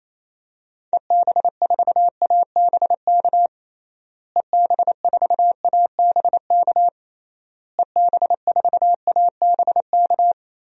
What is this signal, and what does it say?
Morse code, heard as a keyed tone, about 28 wpm, 700 Hz. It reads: EB4ABK EB4ABK EB4ABK
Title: DXCC-100 Call Signs - 6 Characters - Repeated 3x - 28wpm